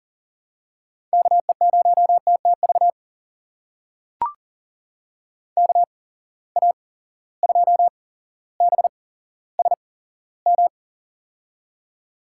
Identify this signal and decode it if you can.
Morse code, heard as a keyed tone, about 40 wpm, spaced as for 17 wpm, 700 Hz. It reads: KE0TTV KA2BSM
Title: USA Call Signs - 6 Characters - 8x Character Spacing - 40wpm